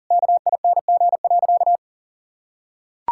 Morse code, heard as a keyed tone, about 40 wpm, 700 Hz. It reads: KING.
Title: Sentences from Top 1000 Words - Review Part 5 - 40wpm